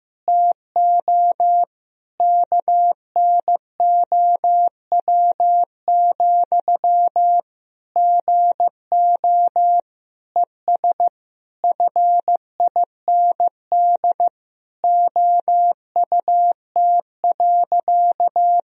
Morse code, heard as a keyed tone, about 15 wpm, 700 Hz. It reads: TO KNOW, GO ES FIND OUT.
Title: Sentences from Top 200 Words and Abbreviations 15wpm